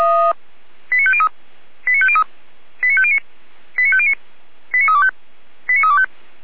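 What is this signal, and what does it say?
Five-tone selective call (ZVEI), heard as quick runs of five tones, tones of about 70 ms. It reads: 89503 89503 89509 89509 89325 89325
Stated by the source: ZVEI five-tone selective call 89305, received via SDR, recorded by Feuerwehr Orlamünde fire brigade.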